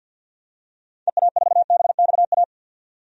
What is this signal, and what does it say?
Morse code, heard as a keyed tone, about 47 wpm, 700 Hz. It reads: EA4BXA
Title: DXCC-100 Call Signs - 6 Characters - 50wpm